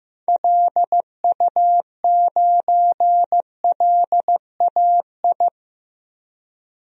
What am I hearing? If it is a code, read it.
Morse code, heard as a keyed tone, about 15 wpm, 700 Hz. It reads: LU9LAI
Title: DXCC-100 Call Signs - 6 Characters - 15wpm